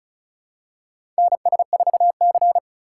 Morse code, heard as a keyed tone, about 36 wpm, 700 Hz. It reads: NS4C
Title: USA Call Signs - 4 Characters - 2x Character Spacing - 35wpm